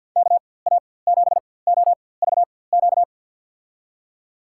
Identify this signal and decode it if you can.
Morse code, heard as a keyed tone, about 52 wpm, spaced as for 23 wpm, 700 Hz. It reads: KA7OVQ
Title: USA Call Signs - 6 Characters - 4x Character Spacing - 50wpm